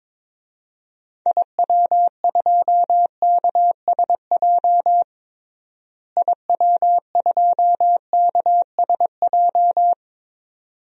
Morse code, heard as a keyed tone, about 22 wpm, 700 Hz. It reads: IW2KSJ IW2KSJ
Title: DXCC-100 Call Signs - 6 Characters - Repeated 2x - 22wpm